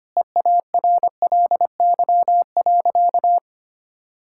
Morse code, heard as a keyed tone, about 25 wpm, 700 Hz. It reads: EARLY.
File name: practice / 25wpm / Sentences with 3-Word Phrases - 25wpm